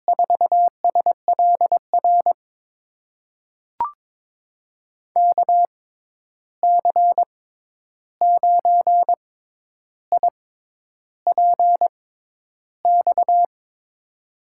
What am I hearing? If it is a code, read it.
Morse code, heard as a keyed tone, about 22 wpm, spaced as for 10 wpm, 700 Hz. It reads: 4SLR KC9IPX
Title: USA Call Signs - 6 Characters - 6x Character Spacing - 22wpm